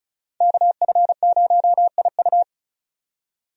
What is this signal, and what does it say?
Morse code, heard as a keyed tone, about 36 wpm, 700 Hz. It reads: KF0IU
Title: USA Call Signs - 5 Characters - 35wpm